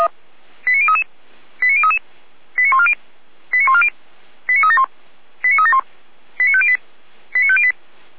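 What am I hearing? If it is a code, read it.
Five-tone selective call (ZVEI), heard as quick runs of five tones, tones of about 70 ms. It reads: 89033 89033 89140 89140 89371 89371 89508 89508
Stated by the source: ZVEI five-tone selective call 89105, received via SDR.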